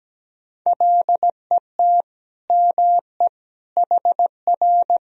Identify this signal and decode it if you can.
Morse code, heard as a keyed tone, about 17 wpm, 700 Hz. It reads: LET ME HR
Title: Sentences from Top 400 Words and Abbreviations 17wpm